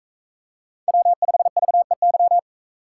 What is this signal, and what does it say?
Morse code, heard as a keyed tone, about 41 wpm, 700 Hz. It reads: W5VEY